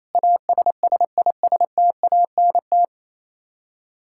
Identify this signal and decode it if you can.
Morse code, heard as a keyed tone, about 28 wpm, 700 Hz. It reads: ASSISTANT
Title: Practice QSO - Exchange 3 - More Info - 28wpm